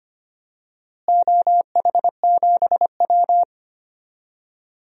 Morse code, heard as a keyed tone, about 25 wpm, 700 Hz. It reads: OH7W
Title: DXCC-100 Call Signs - 4 Characters - Repeated 2x - 25wpm